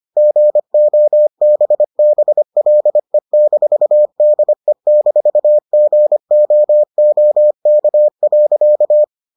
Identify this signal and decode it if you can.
Morse code, heard as a keyed tone, about 25 wpm, 600 Hz. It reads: GOBBLE-DE-GOOK.